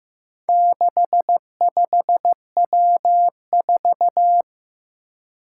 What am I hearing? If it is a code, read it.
Morse code, heard as a keyed tone, about 15 wpm, 700 Hz. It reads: 65W4
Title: Four Challenging Letter-Number Combinations 15wpm